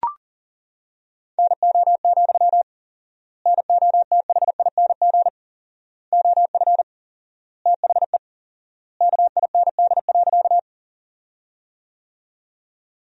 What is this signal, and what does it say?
Morse code, heard as a keyed tone, about 40 wpm, 700 Hz. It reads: NO, NOTHING OF THE KIND.
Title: 4x Word-Spacing - Sentences from Top 500 Words - 40wpm